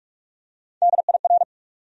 Morse code, heard as a keyed tone, about 44 wpm, 700 Hz. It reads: DIR
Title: USA Call Sign Postfixes - 3 Letters 45wpm